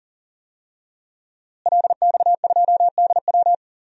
Morse code, heard as a keyed tone, about 39 wpm, 700 Hz. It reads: LX2DW
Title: DXCC-100 Call Signs - 5 Characters - Repeated 2x - 40wpm